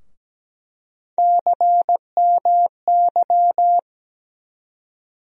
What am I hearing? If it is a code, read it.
Morse code, heard as a keyed tone, about 17 wpm, 700 Hz. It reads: CMY
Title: USA Call Sign Postfixes - 3 Letters 17wpm